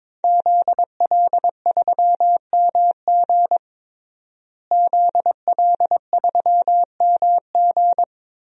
Morse code, heard as a keyed tone, about 22 wpm, 700 Hz. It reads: ZL3MG ZL3MG
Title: DXCC-100 Call Signs - 5 Characters - Repeated 2x - 22wpm